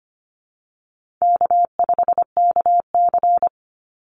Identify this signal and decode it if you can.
Morse code, heard as a keyed tone, about 24 wpm, 700 Hz. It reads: K5KC
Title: Easy Call Signs Repeated 2x - 25wpm